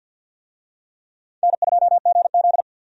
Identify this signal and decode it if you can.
Morse code, heard as a keyed tone, about 51 wpm, 700 Hz. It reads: N2GZ
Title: USA Call Signs - 4 Characters - 50wpm